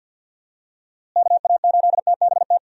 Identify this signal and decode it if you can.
Morse code, heard as a keyed tone, about 50 wpm, 700 Hz. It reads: KA8TBT
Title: USA Call Signs - 6 Characters - Repeated 2x - 50wpm